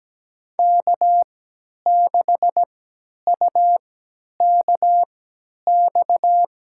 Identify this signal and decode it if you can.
Morse code, heard as a keyed tone, about 17 wpm, spaced as for 10 wpm, 700 Hz. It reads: K6UKX
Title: USA Call Signs - 5 Characters - 3x Character Spacing - 17wpm